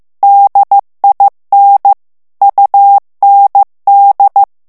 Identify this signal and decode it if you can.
Morse code, heard as a keyed tone, about 15 wpm, 800 Hz. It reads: DIN UND
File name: ALLEMAND texte clair 1 vitesse 14 mots minute